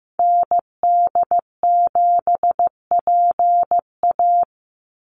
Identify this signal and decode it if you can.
Morse code, heard as a keyed tone, about 15 wpm, 700 Hz. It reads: ND7PA
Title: NA SOTA Activator Callsigns 15wpm